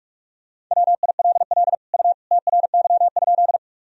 Morse code, heard as a keyed tone, about 43 wpm, 700 Hz. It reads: WILL U TRY?
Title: Sentences from Top 200 Words and Abbreviations 45wpm